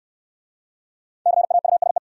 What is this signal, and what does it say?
Morse code, heard as a keyed tone, about 69 wpm, 700 Hz. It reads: CAUSE